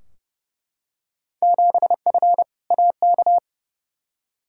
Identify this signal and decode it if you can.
Morse code, heard as a keyed tone, about 30 wpm, 700 Hz. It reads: 7F AK
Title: Field Day Exchanges Encore - 30wpm